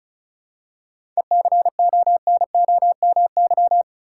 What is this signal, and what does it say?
Morse code, heard as a keyed tone, about 34 wpm, 700 Hz. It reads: ECONOMY
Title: Seven Letter Words - 35wpm